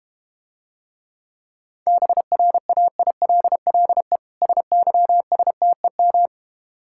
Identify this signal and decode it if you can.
Morse code, heard as a keyed tone, about 32 wpm, 700 Hz. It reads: BRAILLE SYSTEM